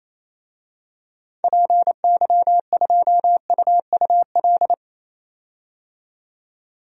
Morse code, heard as a keyed tone, about 28 wpm, 700 Hz. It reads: PY2UUL
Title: DXCC-100 Call Signs - 6 Characters - 28wpm